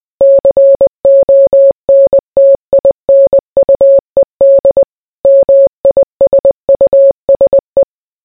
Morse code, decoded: CONTINUED MISUSE